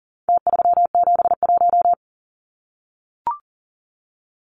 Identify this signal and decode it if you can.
Morse code, heard as a keyed tone, about 40 wpm, 700 Hz. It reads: T371
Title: Numbers - Contest Style - 4-Digit 40wpm